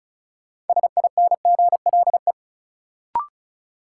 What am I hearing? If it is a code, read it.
Morse code, heard as a keyed tone, about 36 wpm, 700 Hz. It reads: SINGLE
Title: Sentences from Top 700 Words - New Word Drill - 35wpm